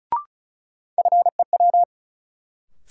Morse code, heard as a keyed tone, about 36 wpm, 700 Hz. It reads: FEW
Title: Sets of 2 Words - New Word Drill - 35wpm